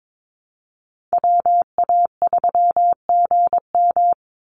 Morse code, heard as a keyed tone, about 22 wpm, 700 Hz. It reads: WA3GM
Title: Call Signs Repeated 2x - 22wpm